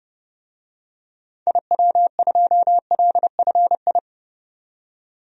Morse code, heard as a keyed tone, about 30 wpm, 700 Hz. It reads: IW2LFI
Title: DXCC-100 Call Signs - 6 Characters - 30wpm